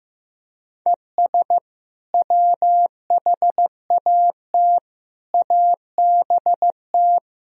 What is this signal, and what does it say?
Morse code, heard as a keyed tone, about 15 wpm, 700 Hz. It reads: ES WHAT ABT
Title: Sentences from Top 200 Words and Abbreviations 15wpm